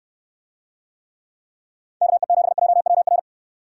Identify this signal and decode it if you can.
Morse code, heard as a keyed tone, about 69 wpm, 700 Hz. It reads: KE7JWR